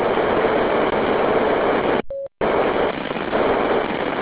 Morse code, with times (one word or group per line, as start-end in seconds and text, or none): none